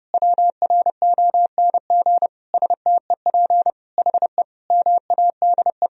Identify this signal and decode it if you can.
Morse code, heard as a keyed tone, about 30 wpm, 700 Hz. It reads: WRONG STEP HE MADE
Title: Sentences from Top 1000 Words - Review Part 2 - 30wpm